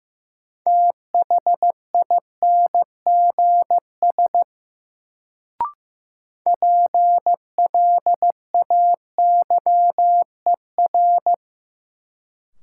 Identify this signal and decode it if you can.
Morse code, heard as a keyed tone, about 15 wpm, 700 Hz. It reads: THINGS PLAYER
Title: Sentences from Top 1000 Words - Review Part 4 - New Word Drill - 15wpm